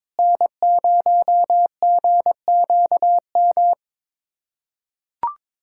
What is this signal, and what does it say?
Morse code, heard as a keyed tone, about 22 wpm, 700 Hz. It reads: N0GQM
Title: USA Callsign DE Callsign - 5 Characters - 22wpm